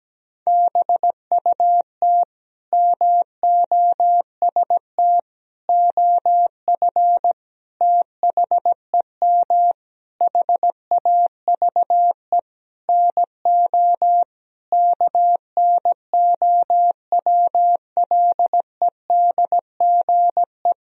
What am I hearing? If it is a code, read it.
Morse code, heard as a keyed tone, about 17 wpm, 700 Hz. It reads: BUT MOST OF THEM HAVE NO KNOWLEDGE
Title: Sentences from Top 1000 Words - Review Part 3 - 17wpm